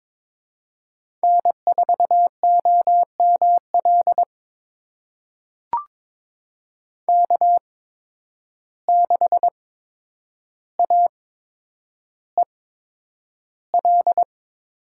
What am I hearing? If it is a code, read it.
Morse code, heard as a keyed tone, about 22 wpm, spaced as for 9 wpm, 700 Hz. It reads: N4OML K6AEL
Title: USA Call Signs - 5 Characters - 8x Character Spacing - 22wpm